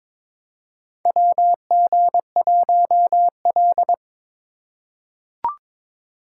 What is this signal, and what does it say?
Morse code, heard as a keyed tone, about 22 wpm, 700 Hz. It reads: WG1L
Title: USA Call Signs - 4 Characters - 2x Character Spacing - 22wpm